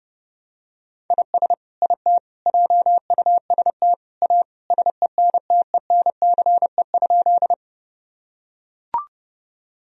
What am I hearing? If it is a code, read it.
Morse code, heard as a keyed tone, about 32 wpm, 700 Hz. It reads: IS IT JUST A SENTENCE?
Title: Sentences from Top 1000 Words - Review Part 1 - 30wpm